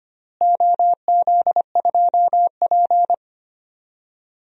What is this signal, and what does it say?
Morse code, heard as a keyed tone, about 25 wpm, 700 Hz. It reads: OZ2P
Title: DXCC-100 Call Signs - 4 Characters - Repeated 3x - 25wpm